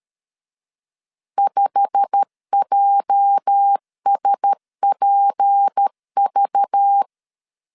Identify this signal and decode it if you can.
Morse code, heard as a keyed tone, about 13 wpm, 800 Hz. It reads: 5JSPV
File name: CW / DET / 1 9 1500C S A 12wpm 3800